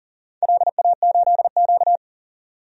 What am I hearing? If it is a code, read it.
Morse code, heard as a keyed tone, about 40 wpm, 700 Hz. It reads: LA8Q
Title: DXCC-100 Call Signs - 4 Characters - 40wpm